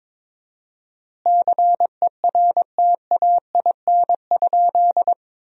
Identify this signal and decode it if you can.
Morse code, heard as a keyed tone, about 21 wpm, 700 Hz. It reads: CERTAIN?
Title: 4x Word-Spacing - Sentences from Top 500 Words - 22wpm